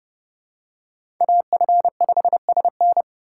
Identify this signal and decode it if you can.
Morse code, heard as a keyed tone, about 28 wpm, 700 Hz. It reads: AF5SN